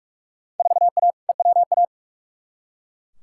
Morse code, heard as a keyed tone, about 44 wpm, 700 Hz. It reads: 4A EWA
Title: Field Day Exchanges - 45wpm